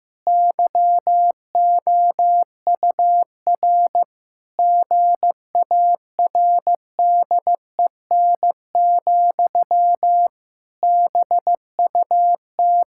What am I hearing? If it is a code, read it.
Morse code, heard as a keyed tone, about 16 wpm, 700 Hz. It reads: YOUR GARDEN, BUT
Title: Pangram Challenge v2 - 15wpm